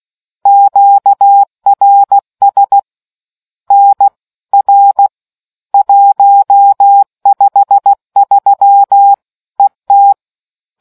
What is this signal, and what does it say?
Morse code, heard as a keyed tone, about 16 wpm, 800 Hz. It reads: QRS NR 153 ET